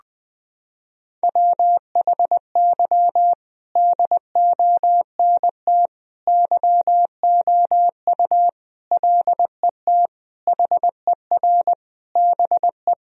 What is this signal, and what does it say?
Morse code, heard as a keyed tone, about 21 wpm, 700 Hz. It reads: WHY DONT YOU LET HER BE
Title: Sentences from Top 1000 Words - Review Part 1 - 20wpm